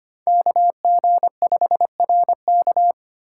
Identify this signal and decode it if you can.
Morse code, heard as a keyed tone, about 25 wpm, 700 Hz. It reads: KG5RK
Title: USA Call Signs - 5 Characters - 2x Character Spacing - 25wpm